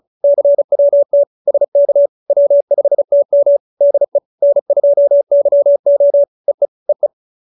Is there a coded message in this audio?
Yes: Morse code from 0.2 to 7.1 s, heard as a keyed tone, about 34 wpm, 570 Hz. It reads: CWT SK W5TM DE N2YO EE EE